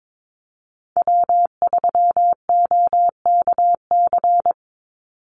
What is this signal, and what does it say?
Morse code, heard as a keyed tone, about 22 wpm, 700 Hz. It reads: W3OKC
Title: Call Signs Repeated 2x - 22wpm